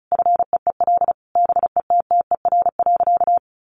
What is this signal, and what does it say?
Morse code, heard as a keyed tone, about 34 wpm, 700 Hz. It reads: FEEL BETTER.
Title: Easy Phrases 35wpm